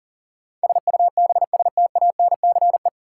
Morse code, heard as a keyed tone, about 38 wpm, 700 Hz. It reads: SUBSTANCE
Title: Sentences from Top 1500 Words - New Word Drill Alt - 40wpm